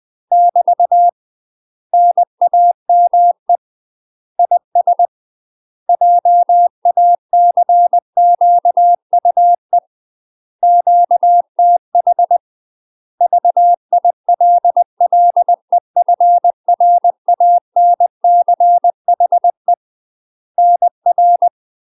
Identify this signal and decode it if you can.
Morse code, heard as a keyed tone, about 20 wpm, 700 Hz. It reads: <BT> NAME IS JACQUE QTH VILLEFRANCHE NR
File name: UK MASTER QSOs TRAINING/20WPM